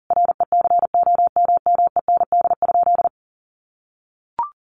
Morse code, heard as a keyed tone, about 40 wpm, 700 Hz. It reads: RECOMMEND?
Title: Sentences from Top 1200 Words 40wpm